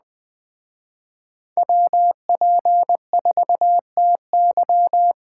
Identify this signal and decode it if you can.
Morse code, heard as a keyed tone, about 21 wpm, 700 Hz. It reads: WP4TY
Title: Speed Racing - USA 2x2 Call Signs 17wpm